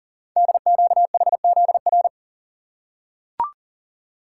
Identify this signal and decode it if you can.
Morse code, heard as a keyed tone, about 42 wpm, 700 Hz. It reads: DQHZR
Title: Code Groups - 5 Letter:Numbers - 40wpm